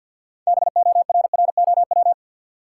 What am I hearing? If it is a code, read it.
Morse code, heard as a keyed tone, about 53 wpm, 700 Hz. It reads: BORROW
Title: Sentences from Top 1500 Words - New Word Drill Alt - 50wpm